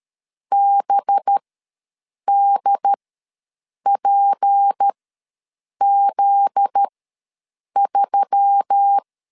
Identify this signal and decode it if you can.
Morse code, heard as a keyed tone, about 13 wpm, spaced as for 7 wpm, 800 Hz. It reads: BDPZ3
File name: LEZIONI / 9 2867 ws 12wpm 3431